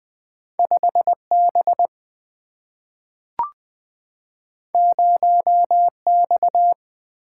Morse code, heard as a keyed tone, about 20 wpm, 700 Hz. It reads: 5 B 0 X
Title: Single-Letter Single-Number 20wpm